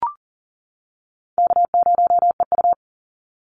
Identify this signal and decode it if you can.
Morse code, heard as a keyed tone, about 40 wpm, 700 Hz. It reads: K0EU
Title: Easy Call Signs - 40wpm